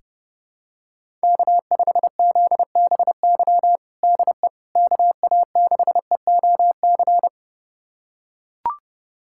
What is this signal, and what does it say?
Morse code, heard as a keyed tone, about 30 wpm, 700 Hz. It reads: K5ZBY DE KA6EOC
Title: USA Callsign DE Callsign 30wpm